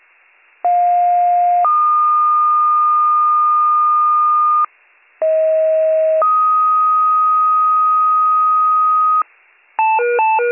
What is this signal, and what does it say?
Two-tone page, heard as a one-second tone, then a three-second tone, about 710 then 1220 Hz, about 640 then 1250 Hz. It is heard from 0.6 to 9.2 s.